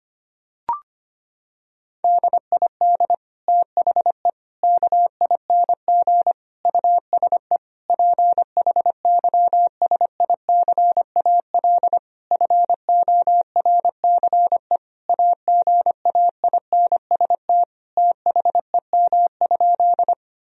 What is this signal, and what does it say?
Morse code, heard as a keyed tone, about 25 wpm, 700 Hz. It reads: DID THE KING USE PHYSICAL FORCE AGAINST THEM?
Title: Sentences from Top 1000 Words - Review Part 4 - 25wpm